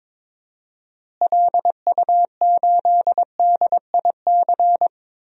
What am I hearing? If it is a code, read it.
Morse code, heard as a keyed tone, about 22 wpm, 700 Hz. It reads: LU8DIC
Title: DXCC-100 Call Signs - 6 Characters - Repeated 2x - 22wpm